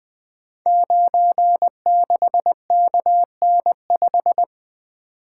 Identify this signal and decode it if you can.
Morse code, heard as a keyed tone, about 21 wpm, 700 Hz. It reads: 96KN5
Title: Code Groups - 5 Letter:Numbers - 20wpm